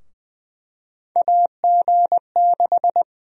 Morse code, heard as a keyed tone, about 20 wpm, 700 Hz. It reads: AG6
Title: USA Call Sign Prefixes - 20wpm